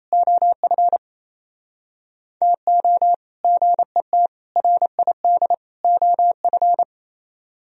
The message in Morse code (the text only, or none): OF TO GET RID OF